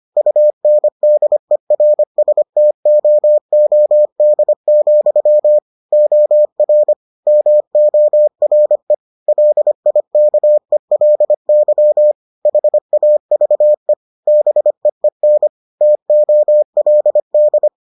Morse code, heard as a keyed tone, about 25 wpm, 600 Hz. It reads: UNDERSTOOD, OR MORE LIKELY HAVE BEEN TOLD